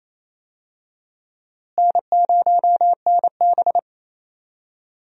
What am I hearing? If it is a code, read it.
Morse code, heard as a keyed tone, about 28 wpm, 700 Hz. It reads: N0NB
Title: USA Call Signs - 4 Characters - Repeated 2x - 28wpm